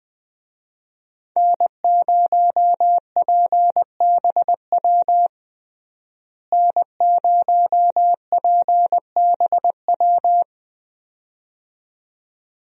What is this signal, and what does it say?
Morse code, heard as a keyed tone, about 20 wpm, 700 Hz. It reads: N0PBW N0PBW
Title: USA Call Signs - 5 Characters - Repeated 2x - 20wpm